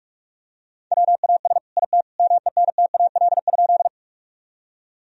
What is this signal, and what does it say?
Morse code, heard as a keyed tone, about 45 wpm, 700 Hz. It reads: WAS IT MENTAL?
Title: Sentences from Top 1000 Words - Review Part 4 - 45wpm